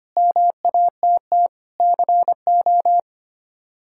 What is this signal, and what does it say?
Morse code, heard as a keyed tone, about 25 wpm, 700 Hz. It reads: MATT CO
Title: K1USN SST QSO Practice - 25wpm